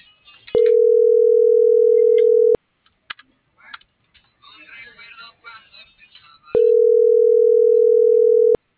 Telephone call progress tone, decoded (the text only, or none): ringback tone